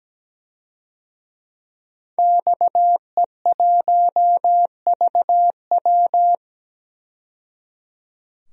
Morse code, heard as a keyed tone, about 17 wpm, 700 Hz. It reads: XE1VW
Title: DXCC-100 Call Signs - 5 Characters - 17wpm